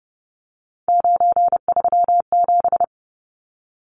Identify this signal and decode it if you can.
Morse code, heard as a keyed tone, about 30 wpm, 700 Hz. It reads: 937
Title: Numbers - 3-Digit 30wpm